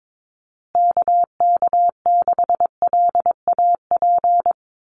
Morse code, heard as a keyed tone, about 22 wpm, 700 Hz. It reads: KK6LAP
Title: USA Call Signs Repeated 2x 22wpm